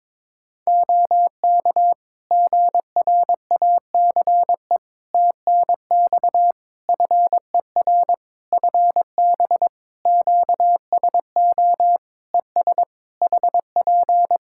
Morse code, heard as a keyed tone, about 22 wpm, 700 Hz. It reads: OK GRACE TNX FER FB QSO ES HP
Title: Practice QSO - Exchange 4 - Ending - 22wpm